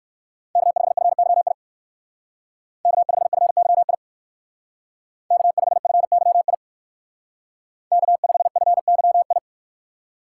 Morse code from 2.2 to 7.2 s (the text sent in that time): K5FYI K5FYI